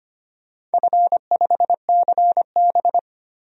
Morse code, heard as a keyed tone, about 25 wpm, 700 Hz. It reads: F5CB
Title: DXCC-100 Call Signs - 4 Characters - 25wpm